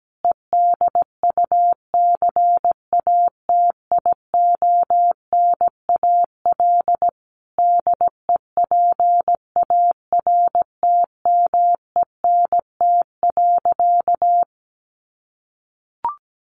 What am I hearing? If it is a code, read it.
Morse code, heard as a keyed tone, about 17 wpm, 700 Hz. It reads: EDUCATIONAL DEPARTMENT.
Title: Sentences from Top 1200 Words 17wpm